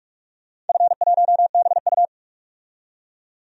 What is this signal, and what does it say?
Morse code, heard as a keyed tone, about 44 wpm, 690 Hz. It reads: F1BU
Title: DXCC-100 Call Signs - 4 Characters - 45wpm